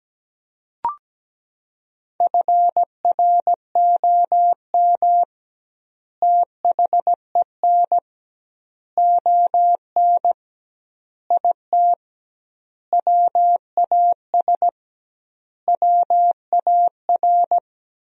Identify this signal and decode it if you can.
Morse code, heard as a keyed tone, about 17 wpm, 700 Hz. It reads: FROM THEN ON IT WAS WAR
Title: 2x Word-Spacing - Sentences from Top 400 Words - 17wpm